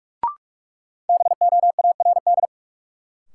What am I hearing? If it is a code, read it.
Morse code, heard as a keyed tone, about 46 wpm, 700 Hz. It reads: BOARD
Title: Sentences from Top 700 Words - New Word Drill - 45wpm